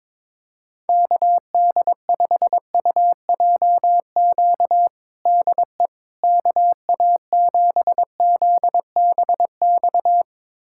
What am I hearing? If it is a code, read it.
Morse code, heard as a keyed tone, about 22 wpm, 700 Hz. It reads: KD5UJQ DE KA7ZBX
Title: USA Callsign DE Callsign - 6 Characters - 22wpm